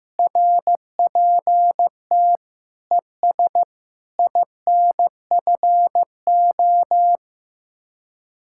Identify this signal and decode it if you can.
Morse code, heard as a keyed tone, about 16 wpm, 700 Hz. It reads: RPT ES INFO
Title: Speed-Racing Common QSO Phrases Encore - 15wpm